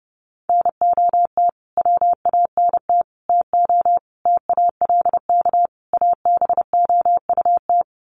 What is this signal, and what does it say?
Morse code, heard as a keyed tone, about 30 wpm, 700 Hz. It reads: NOT WANT TO TALK ABOUT
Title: Sets of 5 Words 30wpm